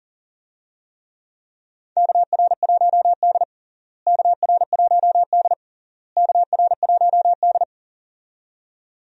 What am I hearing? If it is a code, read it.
Morse code, heard as a keyed tone, about 40 wpm, 700 Hz. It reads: KR1D KR1D KR1D
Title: USA Call Signs - 4 Characters - Repeated 3x - 40wpm